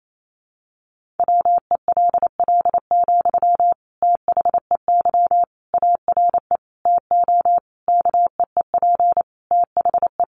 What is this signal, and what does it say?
Morse code, heard as a keyed tone, about 28 wpm, 700 Hz. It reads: WELL, THEY ARE TO KEEP THE
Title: Sentences from Top 1500 Words 28wpm